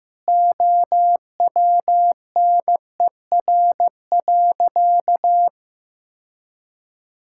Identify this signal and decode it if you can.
Morse code, heard as a keyed tone, about 15 wpm, 700 Hz. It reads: OWNER.
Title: Sentences from Top 1600 Words 15wpm